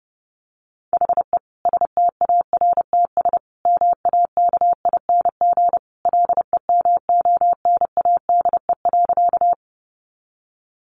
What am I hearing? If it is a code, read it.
Morse code, heard as a keyed tone, about 30 wpm, 700 Hz. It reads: HE STARTS MAKING LEMONADE.
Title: Additional Easy Phrases 30wpm